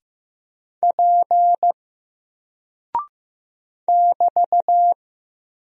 Morse code, heard as a keyed tone, about 15 wpm, 700 Hz. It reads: P <BT>
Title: Single Character (Letter:Number:Punctuation:Prosign) Alt 15wpm